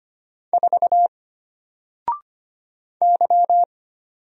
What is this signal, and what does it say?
Morse code, heard as a keyed tone, about 25 wpm, 700 Hz. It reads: 4 Y
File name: Single-Letter-Number-25wpm